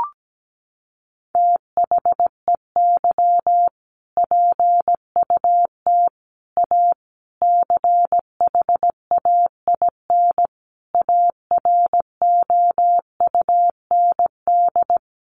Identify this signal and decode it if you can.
Morse code, heard as a keyed tone, about 17 wpm, 700 Hz. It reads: THEY PUT A CHAIN AROUND
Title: Sentences from Top 1500 Words 17wpm